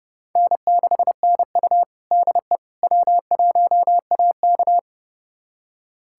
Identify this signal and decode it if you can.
Morse code, heard as a keyed tone, about 30 wpm, 700 Hz. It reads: N6NU DE W1AK
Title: USA Callsign DE Callsign - 4 Characters - 30wpm